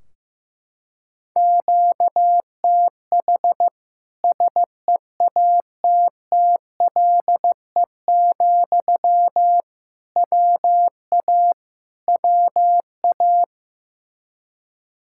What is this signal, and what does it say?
Morse code, heard as a keyed tone, about 15 wpm, 700 Hz. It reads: QTH SEATTLE, WA WA
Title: Practice QSO - Exchange 1 - Beginning - 15wpm